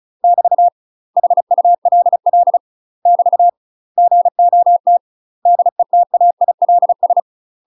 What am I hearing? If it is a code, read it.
Morse code, heard as a keyed tone, about 34 wpm, 700 Hz. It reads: <BT> HULL <BT> GOT DETAILS
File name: UK MASTER QSOs TRAINING / 35WPM / UK MASTER QSOs 0006